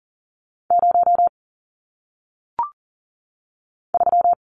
Morse code, heard as a keyed tone, about 39 wpm, 700 Hz. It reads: 0 3
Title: Numbers - 1-Digit 40wpm